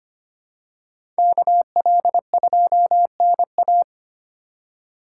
Morse code, heard as a keyed tone, about 25 wpm, 700 Hz. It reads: KL2NA
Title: USA Call Signs - 5 Characters - 4x Character Spacing - 25wpm